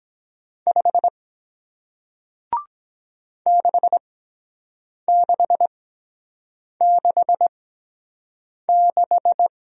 Morse code, heard as a keyed tone, about 23 wpm, 700 Hz. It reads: 5 6 6 6 6